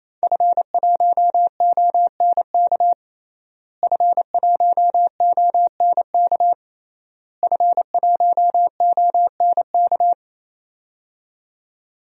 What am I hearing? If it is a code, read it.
Morse code, heard as a keyed tone, about 28 wpm, 700 Hz. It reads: F1ONK F1ONK F1ONK
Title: DXCC-100 Call Signs - 5 Characters - Repeated 3x - 28wpm